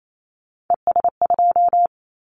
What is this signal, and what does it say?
Morse code, heard as a keyed tone, about 28 wpm, 700 Hz. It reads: ES2
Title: Callsign Prefixes for Top 100 DXCC - 28wpm